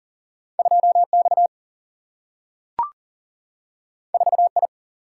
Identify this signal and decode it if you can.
Morse code, heard as a keyed tone, about 40 wpm, 700 Hz. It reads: 2X 4I
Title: Single-Letter Single-Number 40wpm